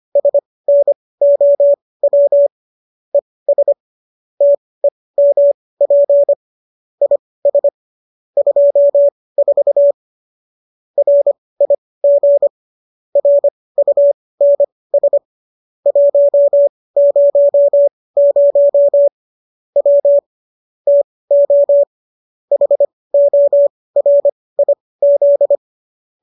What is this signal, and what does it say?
Morse code, heard as a keyed tone, about 25 wpm, spaced as for 18 wpm, 570 Hz. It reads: SNOW ES TEMP IS 24 RIG RUNS 100 W TO HORIZ